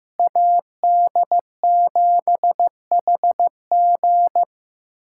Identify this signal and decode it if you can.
Morse code, heard as a keyed tone, about 16 wpm, 700 Hz. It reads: AD7HG
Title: USA Callsign DE Callsign - 5 Characters - 15wpm